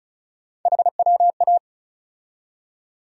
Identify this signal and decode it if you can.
Morse code, heard as a keyed tone, about 36 wpm, 700 Hz. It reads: HWA
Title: USA Call Sign Postfixes - 3 Letters 35wpm